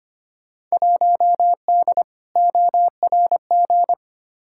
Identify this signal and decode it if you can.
Morse code, heard as a keyed tone, about 25 wpm, 700 Hz. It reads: 1D ORG
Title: Field Day Exchanges - 25wpm